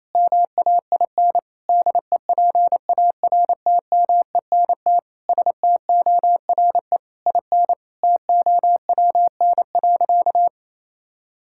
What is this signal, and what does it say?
Morse code, heard as a keyed tone, about 28 wpm, 700 Hz. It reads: MAIN DEPARTMENT STORE IN TOWN.
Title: Sentences from Top 1000 Words - Review Part 5 - 28wpm